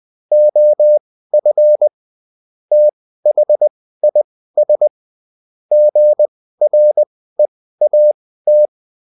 Morse code, heard as a keyed tone, about 20 wpm, spaced as for 14 wpm, 600 Hz. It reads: OF THIS GREAT